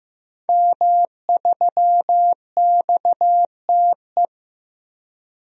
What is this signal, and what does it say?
Morse code, heard as a keyed tone, about 15 wpm, 700 Hz. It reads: M3XTE
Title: DXCC-100 Call Signs - 5 Characters - Repeated 3x - 15wpm